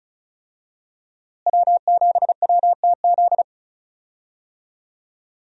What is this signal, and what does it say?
Morse code, heard as a keyed tone, about 36 wpm, 700 Hz. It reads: W7WTZ
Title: USA Call Signs - 5 Characters - Repeated 3x - 35wpm